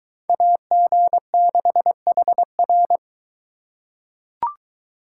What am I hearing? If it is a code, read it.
Morse code, heard as a keyed tone, about 22 wpm, 700 Hz. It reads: AG6HR